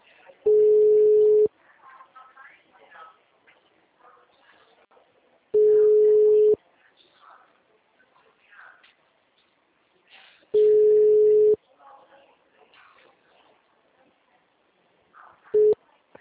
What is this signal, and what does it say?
Ringback tone (Europe, 425 Hz), with speech.